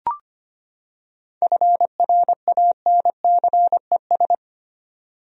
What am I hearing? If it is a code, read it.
Morse code, heard as a keyed tone, about 25 wpm, 700 Hz. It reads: FRANCES
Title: USA Common Names - 25wpm